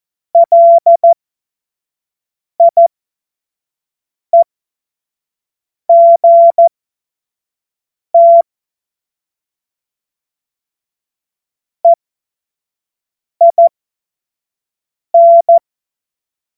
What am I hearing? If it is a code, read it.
Morse code, heard as a keyed tone, about 14 wpm, spaced as for 5 wpm, 680 Hz. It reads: LIEGT EIN
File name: Morsekurs / Witze / Witz 17. Mai 2022 25 F70